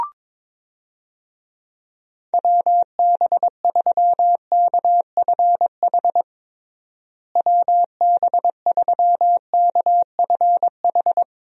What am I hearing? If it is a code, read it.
Morse code, heard as a keyed tone, about 22 wpm, 700 Hz. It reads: WB3KFH WB3KFH
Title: USA Call Signs - 6 Characters - Repeated 2x - 22wpm